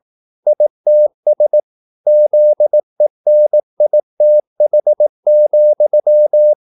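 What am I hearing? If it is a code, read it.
Morse code, heard as a keyed tone, about 18 wpm, 600 Hz. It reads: ITS ZENITH,
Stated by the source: Morse practice at 18 wpm, English text, from CW med Gnister.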